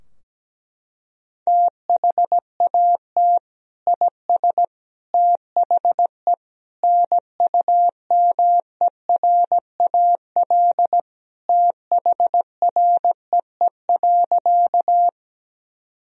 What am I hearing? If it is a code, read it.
Morse code, heard as a keyed tone, about 17 wpm, 700 Hz. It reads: THAT IS THE NUMERAL THREE.
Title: Sentences from Top 1000 Words - Review Part 2 - 17wpm